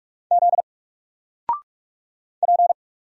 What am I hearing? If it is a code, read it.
Morse code, heard as a keyed tone, about 46 wpm, 700 Hz. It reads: Z P
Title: Single Letter-Number 45wpm